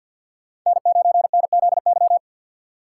Morse code, heard as a keyed tone, about 50 wpm, 690 Hz. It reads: N9NZY
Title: USA Call Signs - 5 Characters - 6x Character Spacing - 50wpm